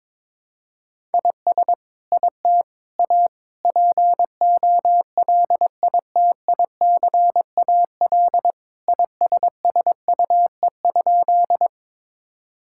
Morse code, heard as a keyed tone, about 22 wpm, 700 Hz. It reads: IS IT A POLITICAL ISSUE?
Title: Sentences from Top 1000 Words - Review Part 6 - 22wpm